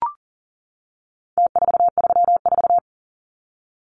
Morse code, heard as a keyed tone, about 40 wpm, 700 Hz. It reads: T434
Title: Numbers - Contest Style - 4-Digit 40wpm